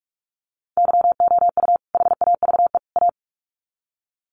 Morse code, heard as a keyed tone, about 44 wpm, 700 Hz. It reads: YOU HAVE A
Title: Sets of 3 Words 45wpm